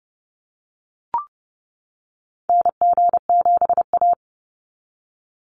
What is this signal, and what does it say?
Morse code, heard as a keyed tone, about 28 wpm, 700 Hz. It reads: NG7A
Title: NA SOTA Activator Callsigns 30wpm